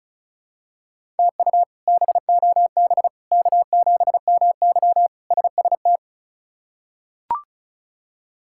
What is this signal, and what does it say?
Morse code, heard as a keyed tone, about 34 wpm, 700 Hz. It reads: TU BOB K7MY SST
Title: K1USN SST QSO Practice - 35wpm